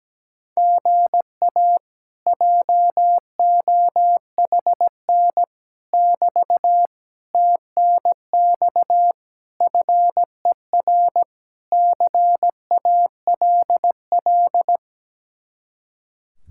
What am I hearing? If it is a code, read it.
Morse code, heard as a keyed tone, about 17 wpm, 700 Hz. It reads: GA JOHN <BT> TNX FER CALL
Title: Speed-Racing Common QSO Phrases - 17wpm